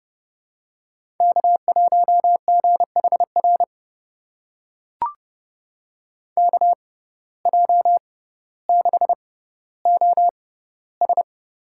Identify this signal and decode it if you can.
Morse code, heard as a keyed tone, about 32 wpm, spaced as for 16 wpm, 700 Hz. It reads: K1GHR KJ6OS
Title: USA Call Signs - 5 Characters - 6x Character Spacing - 30wpm